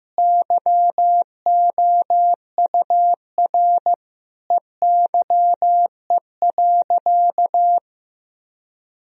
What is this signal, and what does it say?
Morse code, heard as a keyed tone, about 15 wpm, 700 Hz. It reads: YOUR EYE.
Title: Sentences from Top 1000 Words - Review Part 1 - 15wpm